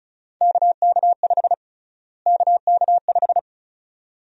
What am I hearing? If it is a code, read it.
Morse code, heard as a keyed tone, about 36 wpm, 700 Hz. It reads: KK5 KK5
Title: USA Call Sign Prefixes - Repeated 2x - 35wpm